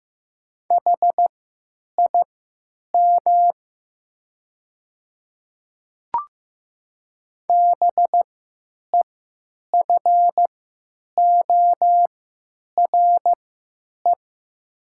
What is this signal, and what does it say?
Morse code, heard as a keyed tone, about 15 wpm, spaced as for 8 wpm, 700 Hz. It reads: HIM BEFORE
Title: 3x Character Spacing - Top 200 Words - 15wpm